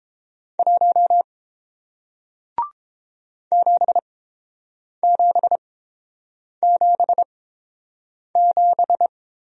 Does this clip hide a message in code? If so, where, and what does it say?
Yes: Morse code from 0.6 to 9.1 s, heard as a keyed tone, about 30 wpm, spaced as for 7 wpm, 700 Hz. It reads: 1 7777